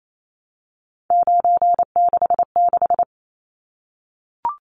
Morse code, heard as a keyed tone, about 28 wpm, 700 Hz. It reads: 966
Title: Numbers - 3-Digit 28wpm